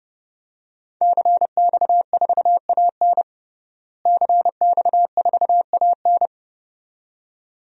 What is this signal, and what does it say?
Morse code, heard as a keyed tone, about 30 wpm, 700 Hz. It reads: CX4AN CX4AN
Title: DXCC-100 Call Signs - 5 Characters - Repeated 2x - 30wpm